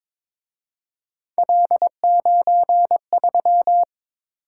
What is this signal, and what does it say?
Morse code, heard as a keyed tone, about 26 wpm, 700 Hz. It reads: L93
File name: Speed-Racing - Three Letter-Number Combinations - 22wpm